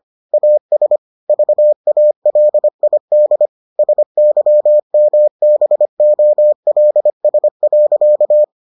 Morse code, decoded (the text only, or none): AS VALID SYMBOLS.